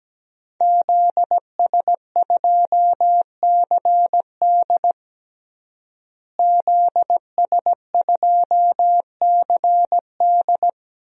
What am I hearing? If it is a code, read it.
Morse code, heard as a keyed tone, about 17 wpm, 700 Hz. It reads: ZS2CD ZS2CD
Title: DXCC-100 Call Signs - 5 Characters - Repeated 2x - 17wpm